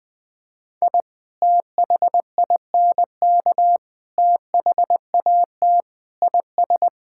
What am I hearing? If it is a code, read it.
Morse code, heard as a keyed tone, about 20 wpm, 700 Hz. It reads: I THINK THAT IS
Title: Sentences with 3-Word Phrases - Spoken Beginning - 20wpm